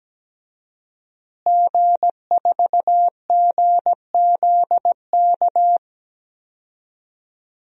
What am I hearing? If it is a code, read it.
Morse code, heard as a keyed tone, about 17 wpm, 700 Hz. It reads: G4GZK
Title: DXCC-100 Call Signs - 5 Characters - Repeated 2x - 17wpm